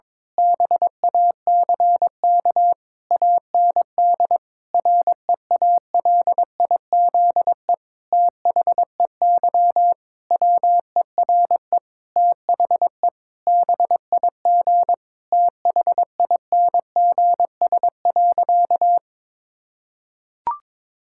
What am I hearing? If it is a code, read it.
Morse code, heard as a keyed tone, about 23 wpm, 700 Hz. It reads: BACK AND REALIZE THEY WERE THE BIG THINGS.
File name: Difficult Sentences with Hints - 22wpm